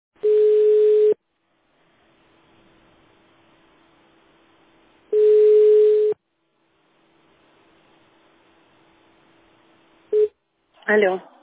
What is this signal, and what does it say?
Ringback tone (Europe, 425 Hz), with speech.